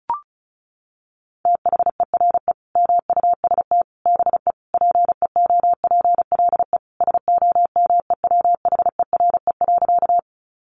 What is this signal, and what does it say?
Morse code, heard as a keyed tone, about 34 wpm, 700 Hz. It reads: THERE MUST BE PEOPLE SOMEWHERE.
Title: Sentences from Top 1400 Words 35wpm